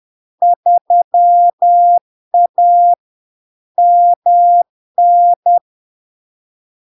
Morse code, heard as a keyed tone, about 10 wpm, 700 Hz. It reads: 3A MN
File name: LARK-FD-2022-Log-in-CW-10wpm-700-Hz